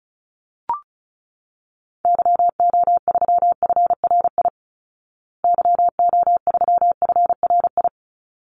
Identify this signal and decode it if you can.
Morse code, heard as a keyed tone, about 34 wpm, 700 Hz. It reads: YO3FRI YO3FRI
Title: Call Signs Repeated 3x - 35wpm